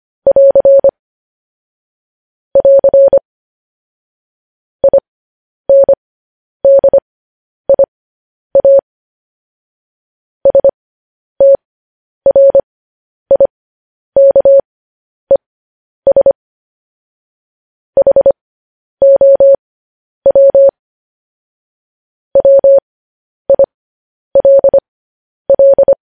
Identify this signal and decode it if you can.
Morse code, heard as a keyed tone, about 25 wpm, spaced as for 10 wpm, 550 Hz. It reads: <AR> <AR> INDIA STRIKES HOW WILL
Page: news-headlines-in-morse-code-at-10-wpm